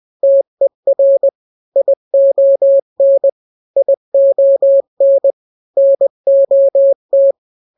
Morse code, heard as a keyed tone, about 20 wpm, 550 Hz. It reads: TER ION ION NOT
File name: n-grams / 3-grams 20 18 WPM